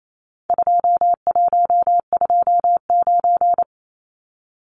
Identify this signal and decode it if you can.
Morse code, heard as a keyed tone, about 28 wpm, 700 Hz. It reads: 2129
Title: Numbers - 4-Digit 28wpm